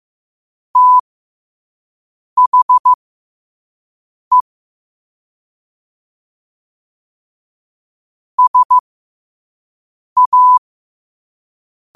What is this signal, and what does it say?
Morse code, heard as a keyed tone, about 15 wpm, spaced as for 5 wpm, 1000 Hz. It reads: THE SA
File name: qotdfiles / 5wpm